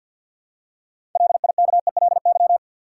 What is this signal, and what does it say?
Morse code, heard as a keyed tone, about 53 wpm, 700 Hz. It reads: LIKELY
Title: Six Letter Words - 50wpm